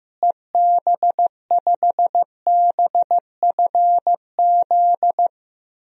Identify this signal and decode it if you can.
Morse code, heard as a keyed tone, about 15 wpm, 700 Hz. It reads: EB5BFZ
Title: DXCC-100 Call Signs - 6 Characters - Repeated 2x - 15wpm